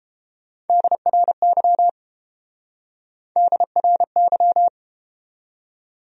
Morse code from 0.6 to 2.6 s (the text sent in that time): DRY